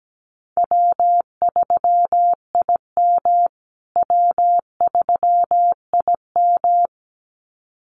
Morse code, heard as a keyed tone, about 17 wpm, 700 Hz. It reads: W3IM W3IM
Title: USA Call Signs Repeated 2x 17wpm